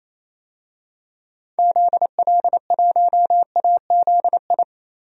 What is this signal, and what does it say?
Morse code, heard as a keyed tone, about 28 wpm, 700 Hz. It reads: ZL1AZI
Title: DXCC-100 Call Signs - 6 Characters - 28wpm